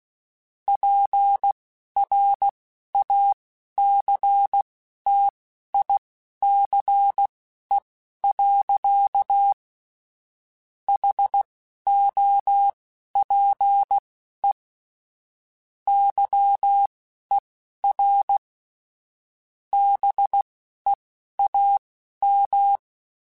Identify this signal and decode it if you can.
Morse code, heard as a keyed tone, about 16 wpm, spaced as for 11 wpm, 800 Hz. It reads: PRACTICE. HOPE YER BEAM
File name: QSOs / QSOs-mp3s / WD0BGP 2@10